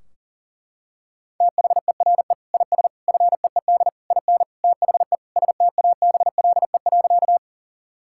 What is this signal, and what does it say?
Morse code, heard as a keyed tone, about 40 wpm, 700 Hz. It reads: THERE IS FEED IN THE STABLE.
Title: Sentences from Top 1600 Words 40wpm